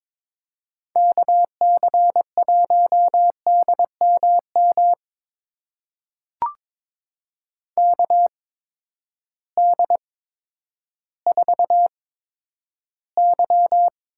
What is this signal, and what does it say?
Morse code, heard as a keyed tone, about 22 wpm, spaced as for 11 wpm, 700 Hz. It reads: KC1DMM KD4Y
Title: USA Call Signs - 6 Characters - 8x Character Spacing - 22wpm